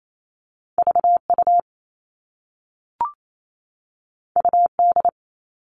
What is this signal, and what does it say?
Morse code, heard as a keyed tone, about 28 wpm, 700 Hz. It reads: VU UD
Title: All Cut Numbers - 2-Digit 28wpm